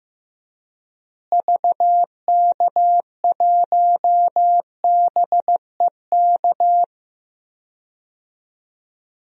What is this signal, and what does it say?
Morse code, heard as a keyed tone, about 15 wpm, 700 Hz. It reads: VK1BEK
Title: DXCC-100 Call Signs - 6 Characters - Repeated 3x - 15wpm